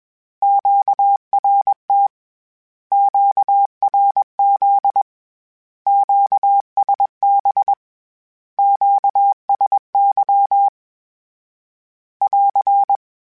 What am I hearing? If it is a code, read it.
Morse code, heard as a keyed tone, about 21 wpm, 800 Hz. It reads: QRT QRZ QSB QSY <AR>
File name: Q Calls / Q Calls-mp3s / Q Calls Once @20